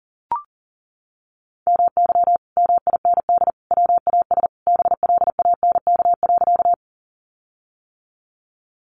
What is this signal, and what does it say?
Morse code, heard as a keyed tone, about 40 wpm, 700 Hz. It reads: MY MIND WAS BLANK.
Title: Sentences from Top 1500 Words 40wpm